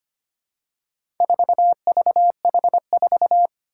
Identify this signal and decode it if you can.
Morse code, heard as a keyed tone, about 25 wpm, 700 Hz. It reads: 4VH4
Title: Four Challenging Letter-Number Combinations 25wpm